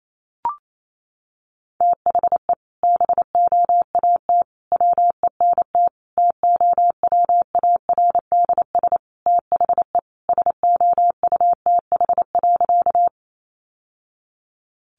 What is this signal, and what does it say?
Morse code, heard as a keyed tone, about 28 wpm, 700 Hz. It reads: THE BOAT WENT TOWARDS THE SOUTH.
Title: Sentences from Top 1100 Words 28wpm